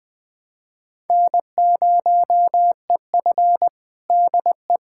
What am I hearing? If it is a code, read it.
Morse code, heard as a keyed tone, about 20 wpm, 700 Hz. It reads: N0EF DE
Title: Practice QSO - Exchange 4 - Ending - 20wpm